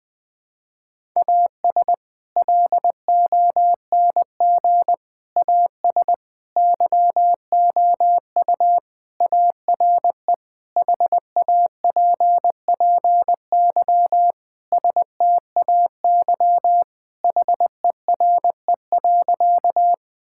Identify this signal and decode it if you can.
Morse code, heard as a keyed tone, about 20 wpm, 700 Hz. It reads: AS LONG AS YOU ARE HAPPY STAY HERE.
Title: Sentences with 4-Word Phrases - 20wpm